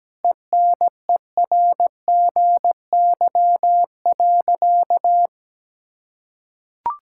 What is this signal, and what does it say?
Morse code, heard as a keyed tone, about 17 wpm, 700 Hz. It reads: ENERGY.
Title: Sentences from Top 1000 Words - Review Part 6 - 17wpm